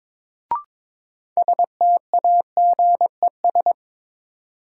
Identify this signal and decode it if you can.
Morse code, heard as a keyed tone, about 22 wpm, 700 Hz. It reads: STAGES
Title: Sets of 5 Words Encore - New Word Drill - 22wpm